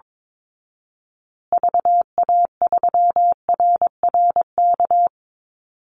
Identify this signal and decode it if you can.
Morse code, heard as a keyed tone, about 22 wpm, 700 Hz. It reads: VA3RRK
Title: Call Signs Repeated 3x - 22wpm